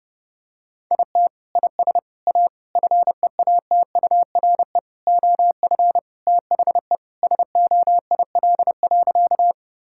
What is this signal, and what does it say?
Morse code, heard as a keyed tone, about 31 wpm, 700 Hz. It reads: IT IS A FEATURE OF THE SOIL.
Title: Sentences from Top 1000 Words - Review Part 5 - 30wpm